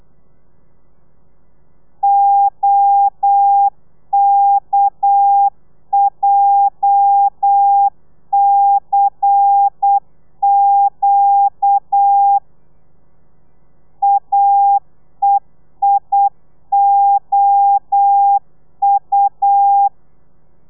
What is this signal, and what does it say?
Morse code, heard as a keyed tone, about 8 wpm, 800 Hz. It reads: OKJCQ AEIOU
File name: telegrafia / i2tfj / Lez11 35